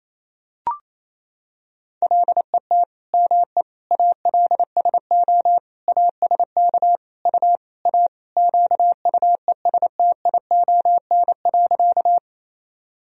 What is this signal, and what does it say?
Morse code, heard as a keyed tone, about 27 wpm, 700 Hz. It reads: LET ME ALSO ASK U A QUESTION.
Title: Sentences from Top 400 Words and Abbreviations 28wpm